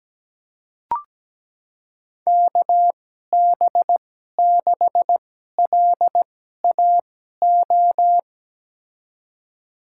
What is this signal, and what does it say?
Morse code, heard as a keyed tone, about 17 wpm, spaced as for 12 wpm, 700 Hz. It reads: KB6LAO
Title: USA Call Signs - 6 Characters - 2x Character Spacing - 17wpm